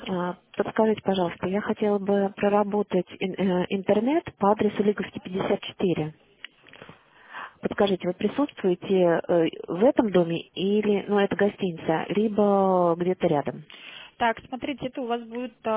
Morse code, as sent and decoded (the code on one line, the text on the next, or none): none